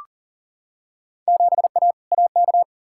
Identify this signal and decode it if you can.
Morse code, heard as a keyed tone, about 40 wpm, 700 Hz. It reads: 7A AK
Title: Field Day Exchanges Encore - 40wpm